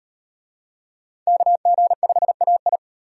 Morse code, heard as a keyed tone, about 34 wpm, 700 Hz. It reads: KG5AI